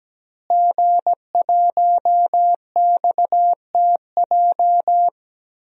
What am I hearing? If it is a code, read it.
Morse code, heard as a keyed tone, about 17 wpm, 700 Hz. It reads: G1XTJ
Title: DXCC-100 Call Signs - 5 Characters - Repeated 3x - 17wpm